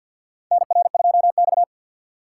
Morse code, heard as a keyed tone, about 52 wpm, 700 Hz. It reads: NR2X
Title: USA Call Signs - 4 Characters - 6x Character Spacing - 50wpm